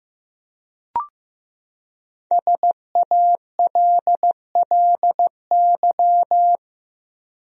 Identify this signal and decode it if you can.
Morse code, heard as a keyed tone, about 15 wpm, 700 Hz. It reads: SALLY
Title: USA Common Names - Max 5 Letters - 15wpm